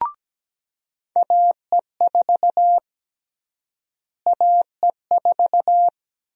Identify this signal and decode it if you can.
Morse code, heard as a keyed tone, about 17 wpm, 700 Hz. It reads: AE4 AE4
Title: USA Call Sign Prefixes - Repeated 2x - 17wpm